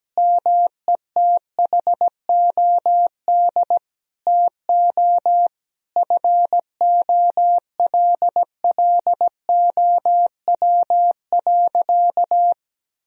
Morse code, heard as a keyed tone, about 17 wpm, 700 Hz. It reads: METHOD TO FOLLOW.
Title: Sentences from Top 1000 Words - Review Part 3 - 17wpm